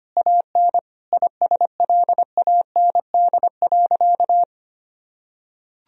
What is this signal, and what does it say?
Morse code, heard as a keyed tone, about 25 wpm, 700 Hz. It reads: AN ISLAND.
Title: Sentences from Top 1000 Words - Review Part 3 - 25wpm